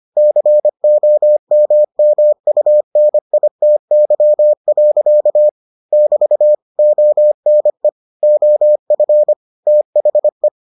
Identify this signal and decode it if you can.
Morse code, heard as a keyed tone, about 25 wpm, 600 Hz. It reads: COMMUNITY. <BT> ONE OF THE